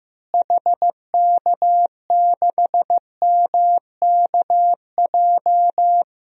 Morse code, heard as a keyed tone, about 15 wpm, 700 Hz. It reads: HK6MKJ
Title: DXCC-100 Call Signs - 6 Characters - Repeated 3x - 15wpm